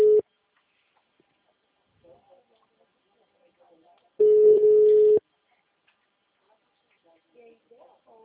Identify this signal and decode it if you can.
Ringback tone (Europe, 425 Hz), with speech.
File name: out- -2049- - - .32017